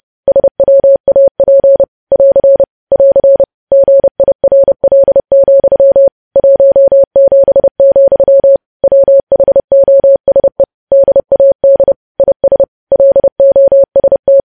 Morse code, heard as a keyed tone, about 30 wpm, 550 Hz. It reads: SWAP <AR> <AR> GIRL, 17, WHOSE DAD IS LOST
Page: news headlines in morse code at 30 wpm